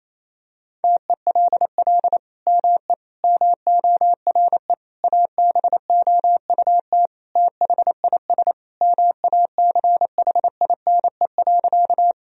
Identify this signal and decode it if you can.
Morse code, heard as a keyed tone, about 28 wpm, 700 Hz. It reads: TELL ME MORE ABOUT THIS MACHINE.
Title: Sentences from Top 1000 Words - Review Part 2 - 28wpm